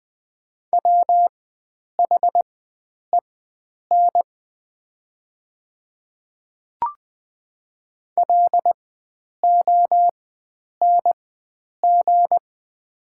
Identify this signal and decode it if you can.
Morse code, heard as a keyed tone, about 20 wpm, spaced as for 9 wpm, 700 Hz. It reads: WHEN LONG
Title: 4x Character Spacing - Top 100 Words - 20wpm